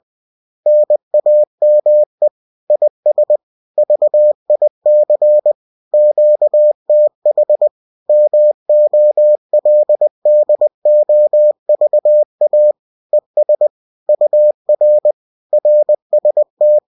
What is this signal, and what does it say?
Morse code, heard as a keyed tone, about 20 wpm, 600 Hz. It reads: NAME IS VIC QTH MOLDOVA ES UR RST